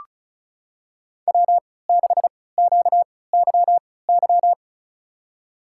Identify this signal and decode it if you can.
Morse code, heard as a keyed tone, about 36 wpm, spaced as for 20 wpm, 700 Hz. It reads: W6QYY
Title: USA Call Signs - 3x Character Spacing - 35wpm